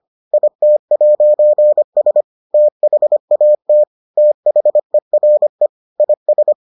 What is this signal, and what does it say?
Morse code, heard as a keyed tone, about 25 wpm, 600 Hz. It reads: IT'S THAT THERE IS